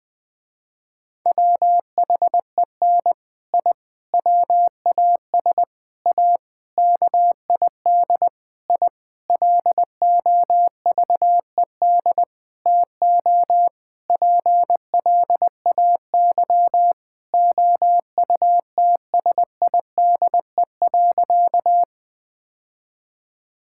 Morse code, heard as a keyed tone, about 20 wpm, 700 Hz. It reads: WHEN I WAS A KID I LOVED TO PLAY OUTSIDE.
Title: Sentences with 5-Word Phrases - Spoken Beginning - 20wpm